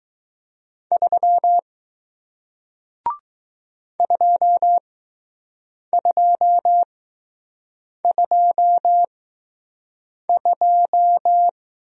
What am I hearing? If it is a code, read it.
Morse code, heard as a keyed tone, about 20 wpm, spaced as for 6 wpm, 700 Hz. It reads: 3 2222